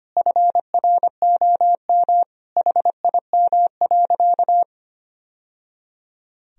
Morse code, heard as a keyed tone, about 25 wpm, 700 Hz. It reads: FROM HIM.
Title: Sentences from Top 1000 Words - Review Part 2 - 25wpm